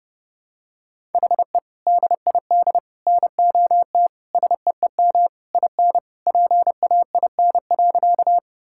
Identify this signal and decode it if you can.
Morse code, heard as a keyed tone, about 30 wpm, 700 Hz. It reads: HE DID NOT SEEM IN PAIN.
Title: Sentences from Top 1600 Words 30wpm